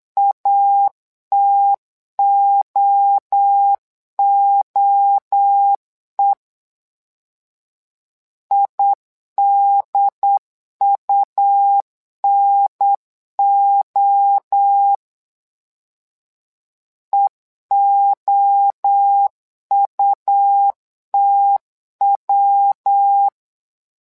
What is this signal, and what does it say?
Morse code, heard as a keyed tone, about 9 wpm, 800 Hz. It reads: ATOOE IDUNO EOUTW
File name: LEZIONI / 3 MWRO / 3 MWRO 8wpm 2606